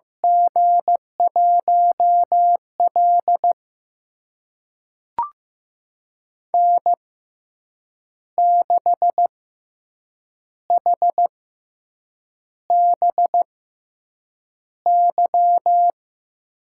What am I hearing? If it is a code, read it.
Morse code, heard as a keyed tone, about 16 wpm, spaced as for 7 wpm, 700 Hz. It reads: G1L N6HBY
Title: USA Call Signs - 6x Character Spacing - 15wpm